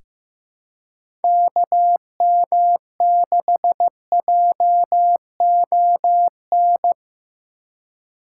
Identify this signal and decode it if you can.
Morse code, heard as a keyed tone, about 15 wpm, 700 Hz. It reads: KM6JON
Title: USA Call Signs - 6 Characters - 4x Character Spacing - 15wpm